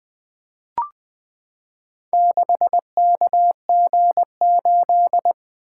Morse code, heard as a keyed tone, about 20 wpm, 700 Hz. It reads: 6KG8
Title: Four Letter-Number Combinations - 20wpm